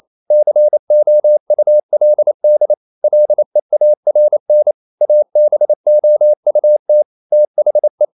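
Morse code, heard as a keyed tone, about 28 wpm, 600 Hz. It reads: COULD LEARN ABOUT THE
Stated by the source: Morse code practice at 28 wpm, text of Ken Follett's novel Never in English, from CW med Gnister.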